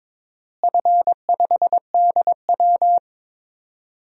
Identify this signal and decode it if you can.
Morse code, heard as a keyed tone, about 22 wpm, 700 Hz. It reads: F5DW
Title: DXCC-100 Call Signs - 4 Characters - Repeated 3x - 22wpm